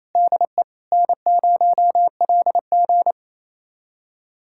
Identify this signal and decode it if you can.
Morse code, heard as a keyed tone, about 28 wpm, 700 Hz. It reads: DE N0LG
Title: USA Callsign DE Callsign - 4 Characters - 28wpm